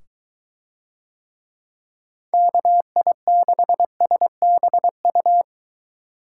Morse code, heard as a keyed tone, about 21 wpm, 700 Hz. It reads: KI6SBU